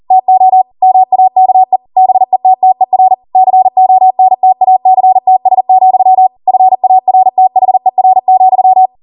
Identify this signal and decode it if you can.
Morse code, heard as a keyed tone, about 40 wpm, 750 Hz. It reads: TO MAKE BETTER CONTACTS, FARTHER,
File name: Morse / 40WPM